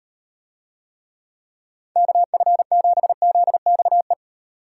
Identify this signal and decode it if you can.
Morse code, heard as a keyed tone, about 38 wpm, 700 Hz. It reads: KF7ZXE